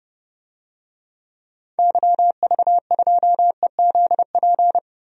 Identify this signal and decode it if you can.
Morse code, heard as a keyed tone, about 30 wpm, 700 Hz. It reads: YV2EZP
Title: DXCC-100 Call Signs - 6 Characters - Repeated 3x - 30wpm